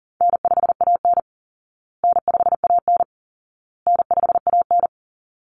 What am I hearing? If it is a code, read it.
Morse code, heard as a keyed tone, about 40 wpm, 700 Hz. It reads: N5AN N5AN N5AN
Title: Call Signs Repeated 2x - 40wpm